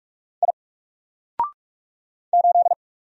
Morse code, heard as a keyed tone, about 44 wpm, 700 Hz. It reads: I 8
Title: Single Character (Letter:Number:Punctuation:Prosign) 45wpm